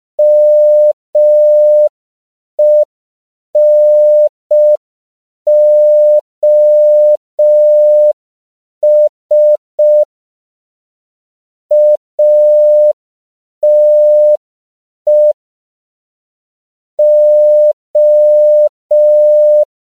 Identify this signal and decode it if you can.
Morse code, heard as a keyed tone, about 5 wpm, 600 Hz. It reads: MENOS ATE O
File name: CW / 5PPM